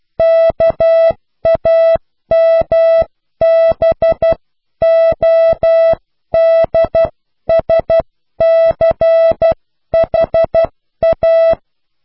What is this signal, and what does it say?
Morse code, heard as a keyed tone, about 12 wpm, 660 Hz. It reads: KAMBODSCHA
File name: cw / pseudo worden(10)